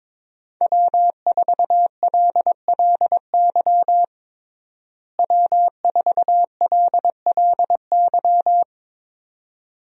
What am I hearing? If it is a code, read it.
Morse code, heard as a keyed tone, about 22 wpm, 700 Hz. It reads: W4LLY W4LLY
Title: USA Call Signs - 5 Characters - Repeated 2x - 22wpm